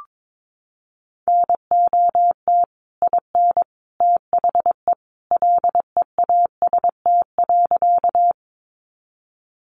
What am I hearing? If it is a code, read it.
Morse code, heard as a keyed tone, about 22 wpm, 700 Hz. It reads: NOT IN THE LEAST.
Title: Sentences from Top 700 Words 22wpm